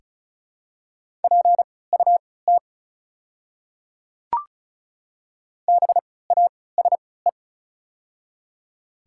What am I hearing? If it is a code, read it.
Morse code, heard as a keyed tone, about 36 wpm, spaced as for 20 wpm, 700 Hz. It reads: PUT BASE
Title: 3x Character Spacing - Top 300 Words - 35wpm